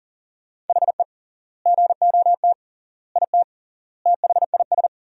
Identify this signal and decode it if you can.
Morse code, heard as a keyed tone, about 40 wpm, 700 Hz. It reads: HE GOT IT THIS
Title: 3x Word-Spacing - Sentences from Top 400 Words - 40wpm